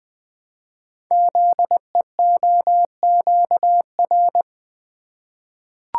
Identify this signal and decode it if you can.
Morse code, heard as a keyed tone, about 20 wpm, 700 Hz. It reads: ZEOQR
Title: Code Groups - 5 Letters - 20wpm